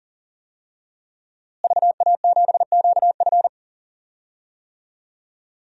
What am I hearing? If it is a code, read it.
Morse code, heard as a keyed tone, about 37 wpm, 700 Hz. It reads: VA7QF
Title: DXCC-100 Call Signs - 5 Characters - Repeated 3x - 40wpm